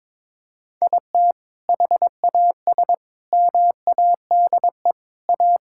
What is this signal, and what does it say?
Morse code, heard as a keyed tone, about 21 wpm, 700 Hz. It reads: IT HAS MADE A
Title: Sentences from Top 1000 Words - Review Part 5 - 22wpm